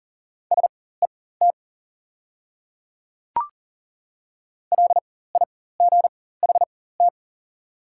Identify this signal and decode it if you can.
Morse code, heard as a keyed tone, about 40 wpm, spaced as for 19 wpm, 700 Hz. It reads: SET LIGHT
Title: 4x Character Spacing - Top 300 Words - 40wpm